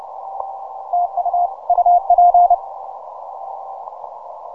Morse code, heard as a keyed tone, about 30 wpm, 710 Hz. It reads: TU UP
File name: simple-file-list / VP8SGI 12MCW